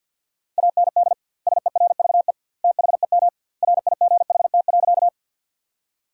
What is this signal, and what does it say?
Morse code, heard as a keyed tone, about 50 wpm, 700 Hz. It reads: AND SERVE THEM RIGHT.
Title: 2x Word-Spacing - Sentences from Top 400 Words - 50wpm